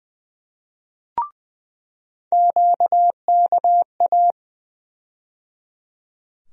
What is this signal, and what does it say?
Morse code, heard as a keyed tone, about 20 wpm, 700 Hz. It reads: QKA
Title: USA Call Sign Postfixes - 3 Letters 20wpm